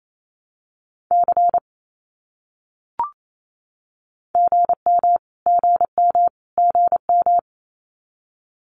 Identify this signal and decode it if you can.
Morse code, heard as a keyed tone, about 28 wpm, 700 Hz. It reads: C GM GM GM
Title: Abbreviations Repeated 3x - 28wpm